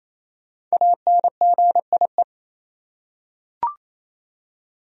Morse code, heard as a keyed tone, about 28 wpm, 700 Hz. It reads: ANGIE
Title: USA Common Names - Max 5 Letters - 28wpm